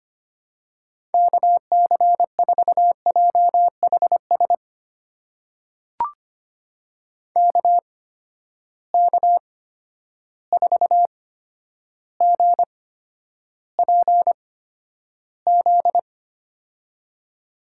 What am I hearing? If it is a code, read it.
Morse code, heard as a keyed tone, about 25 wpm, spaced as for 11 wpm, 700 Hz. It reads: KC4JHS KK4GPZ
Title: USA Call Signs - 6 Characters - 8x Character Spacing - 25wpm